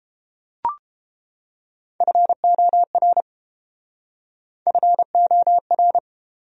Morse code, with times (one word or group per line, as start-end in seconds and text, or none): 2.0-3.2 FOR
4.7-6.0 FOR